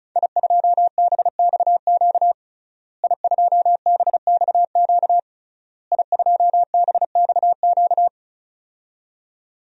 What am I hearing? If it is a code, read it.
Morse code, heard as a keyed tone, about 36 wpm, 700 Hz. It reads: I2BXQ I2BXQ I2BXQ
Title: DXCC-100 Call Signs - 5 Characters - Repeated 3x - 35wpm